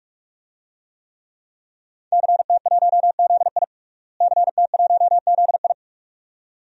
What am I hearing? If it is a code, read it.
Morse code, heard as a keyed tone, about 41 wpm, 700 Hz. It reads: CT1ZI CT1ZI
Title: DXCC-100 Call Signs - 5 Characters - Repeated 2x - 45wpm